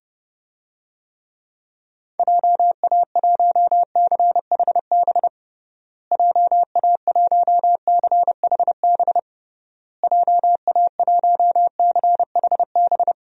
Morse code, heard as a keyed tone, about 32 wpm, 700 Hz. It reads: JA1CHB JA1CHB JA1CHB
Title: DXCC-100 Call Signs - 6 Characters - Repeated 3x - 30wpm